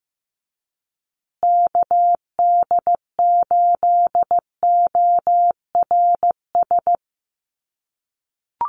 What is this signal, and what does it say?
Morse code, heard as a keyed tone, about 15 wpm, 700 Hz. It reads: KD8ORS
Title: USA Call Signs Repeated 2x 15wpm